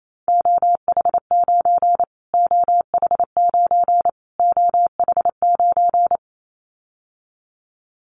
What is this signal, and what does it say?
Morse code, heard as a keyed tone, about 28 wpm, 700 Hz. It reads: OH9 OH9 OH9
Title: Callsign Prefixes for Top 25 DXCC - Repeated 3x - 28wpm